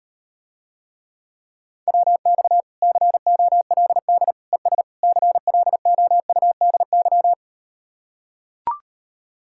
Morse code, heard as a keyed tone, about 39 wpm, 700 Hz. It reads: WX COLD ES CLOUDY